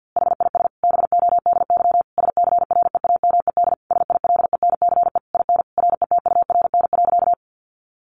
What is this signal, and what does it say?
Morse code, heard as a keyed tone, about 50 wpm, 700 Hz. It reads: HIS BODY SCREAMED SILENCE IN RETURN.
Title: A Sound Of Thunder 50wpm